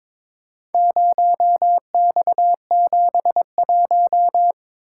Morse code, heard as a keyed tone, about 23 wpm, 700 Hz. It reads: 0X71
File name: practice / 22wpm / Four Letter-Number Combinations - 22wpm